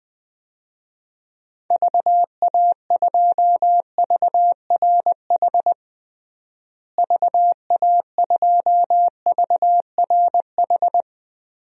Morse code, heard as a keyed tone, about 21 wpm, 700 Hz. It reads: VA2VRH VA2VRH
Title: DXCC-100 Call Signs - 6 Characters - Repeated 3x - 20wpm